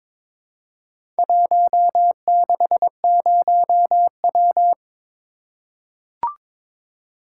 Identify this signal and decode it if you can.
Morse code, heard as a keyed tone, about 22 wpm, 690 Hz. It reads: 160W
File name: practice / 22wpm / Four Letter-Number Combinations - 22wpm